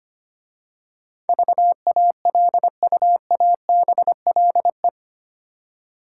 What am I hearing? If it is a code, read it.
Morse code, heard as a keyed tone, about 25 wpm, 700 Hz. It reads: VALUABLE
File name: 8 Letter Words - First 3 Letters Given - 25wpm